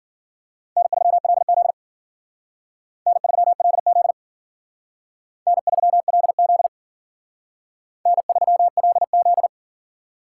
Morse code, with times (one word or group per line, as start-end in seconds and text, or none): 0.8-1.7 N3LZ
3.1-4.1 N3LZ
5.5-6.7 N3LZ
8.1-9.5 N3LZ